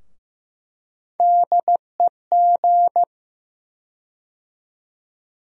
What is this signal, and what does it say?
Morse code, heard as a keyed tone, about 15 wpm, 700 Hz. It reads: DEG
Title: USA Call Sign Postfixes - 3 Letters 15wpm